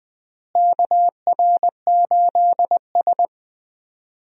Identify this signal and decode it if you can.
Morse code, heard as a keyed tone, about 20 wpm, 700 Hz. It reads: KR8S
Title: USA Call Signs - 4 Characters - Repeated 3x - 20wpm